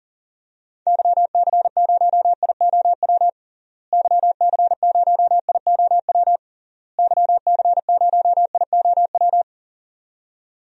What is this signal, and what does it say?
Morse code, heard as a keyed tone, about 40 wpm, 700 Hz. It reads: YC0IOW YC0IOW YC0IOW
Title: DXCC-100 Call Signs - 6 Characters - Repeated 3x - 40wpm